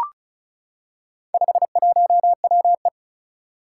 Morse code, heard as a keyed tone, about 36 wpm, 700 Hz. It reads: 51WE
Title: Four Challenging Letter-Number Combinations 35wpm